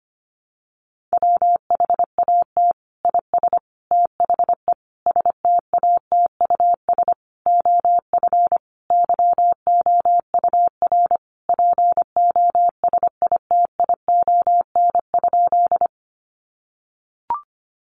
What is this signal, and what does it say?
Morse code, heard as a keyed tone, about 25 wpm, 700 Hz. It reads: WHAT IS THE STATUS OF YOUR POSITION?
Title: Sentences from Top 1400 Words 25wpm